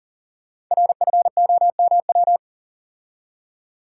Morse code, heard as a keyed tone, about 42 wpm, 700 Hz. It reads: RFOMW
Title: Code Groups - 5 Letters - 40wpm